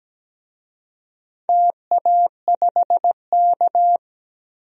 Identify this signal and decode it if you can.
Morse code, heard as a keyed tone, about 17 wpm, 700 Hz. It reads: TA5K
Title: DXCC-100 Call Signs - 4 Characters - 17wpm